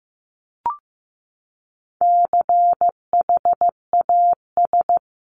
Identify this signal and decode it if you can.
Morse code, heard as a keyed tone, about 14 wpm, 700 Hz. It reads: CHAS
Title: CWT Exchange 15wpm